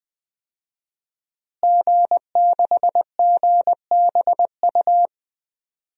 Morse code, heard as a keyed tone, about 20 wpm, 700 Hz. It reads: G6GBU
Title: DXCC-100 Call Signs - 5 Characters - Repeated 3x - 20wpm